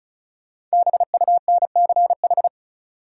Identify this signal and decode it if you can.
Morse code, heard as a keyed tone, about 36 wpm, 700 Hz. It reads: BUNCH
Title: Five Letter Words - 35wpm